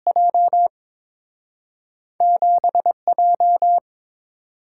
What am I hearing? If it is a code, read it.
Morse code, heard as a keyed tone, about 30 wpm, spaced as for 26 wpm, 700 Hz. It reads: J 7J
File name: Speed-Racing - Two Letter-Number Combinations - 22wpm